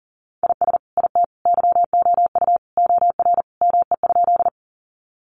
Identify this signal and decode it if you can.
Morse code, heard as a keyed tone, about 40 wpm, 700 Hz. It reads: IS IT YOU OR ME?
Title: Sentences from Top 200 Words 40wpm